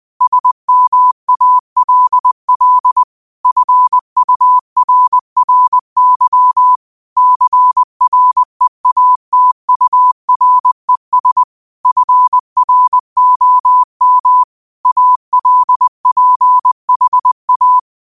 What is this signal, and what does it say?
Morse code, heard as a keyed tone, about 20 wpm, 1000 Hz. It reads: SMALL FURRY CREATURES FROM ALPHA
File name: qotdfiles / 20wpm